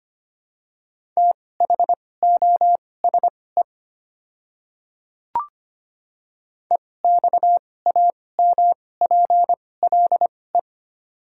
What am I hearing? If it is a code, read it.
Morse code, heard as a keyed tone, about 25 wpm, spaced as for 18 wpm, 690 Hz. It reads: THOSE EXAMPLE
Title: 2x Character Spacing - Top 300 Words - 25wpm